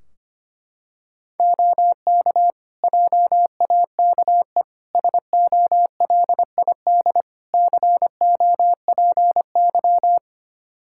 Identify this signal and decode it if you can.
Morse code, heard as a keyed tone, about 25 wpm, 700 Hz. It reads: OK JAKE SOLID COPY
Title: Practice QSO - Exchange 3 - More Info - 25wpm